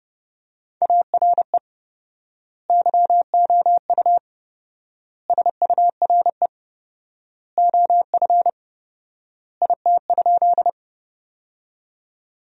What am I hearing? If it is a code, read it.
Morse code, heard as a keyed tone, about 30 wpm, 700 Hz. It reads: ARE YOU SURE OF IT?
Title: 4x Word-Spacing - Sentences from Top 300 Words - 30wpm